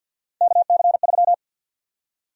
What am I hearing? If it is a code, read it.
Morse code, heard as a keyed tone, about 48 wpm, 700 Hz. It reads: KC3
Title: USA Call Sign Prefixes - 50wpm